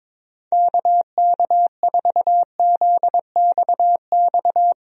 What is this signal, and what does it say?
Morse code, heard as a keyed tone, about 22 wpm, 700 Hz. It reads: KK4ZXX
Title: USA Call Signs - 6 Characters - 22wpm